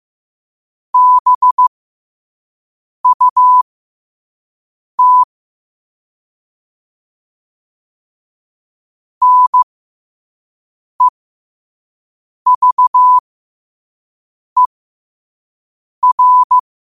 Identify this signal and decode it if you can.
Morse code, heard as a keyed tone, about 15 wpm, spaced as for 5 wpm, 1000 Hz. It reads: BUT NEVER
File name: qotdfiles / 5wpm